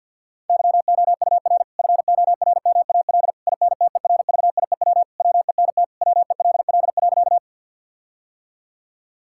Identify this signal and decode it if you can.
Morse code, heard as a keyed tone, about 50 wpm, 700 Hz. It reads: YOUR FORMAL INTERVIEW WENT WELL.
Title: Sentences from Top 1600 Words 50wpm